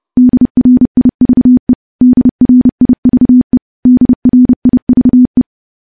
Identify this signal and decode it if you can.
Morse code, heard as a keyed tone, about 30 wpm, 250 Hz. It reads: DRIVE DRIVE DRIVE
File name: morse / morse-training-30WPM